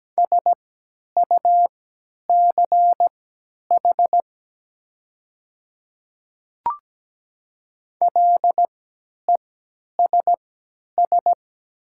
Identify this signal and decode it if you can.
Morse code, heard as a keyed tone, about 17 wpm, spaced as for 10 wpm, 700 Hz. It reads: SUCH LESS
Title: 3x Character Spacing - Top 500 Words - 17wpm